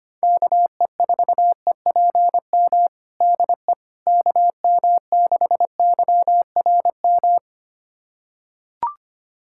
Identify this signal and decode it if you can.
Morse code, heard as a keyed tone, about 25 wpm, 700 Hz. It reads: KE4EPM DE KM6YRM
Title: USA Callsign DE Callsign - 6 Characters - 25wpm